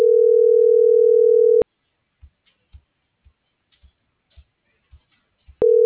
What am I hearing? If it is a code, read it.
Ringback tone (North America), 440 + 480 Hz.